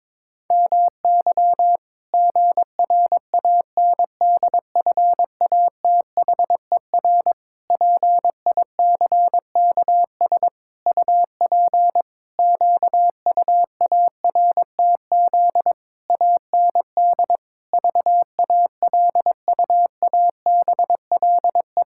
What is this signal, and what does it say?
Morse code, heard as a keyed tone, about 22 wpm, 700 Hz. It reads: MY GRANDFATHER PICKS UP QUARTZ AND VALUABLE
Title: Pangram Challenge v2 - 22wpm